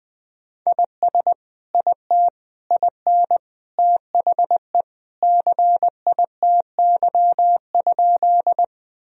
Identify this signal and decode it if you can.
Morse code, heard as a keyed tone, about 20 wpm, 700 Hz. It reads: IS IT IN THE CITY?
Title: Sentences from Top 1000 Words - Review Part 1 - 20wpm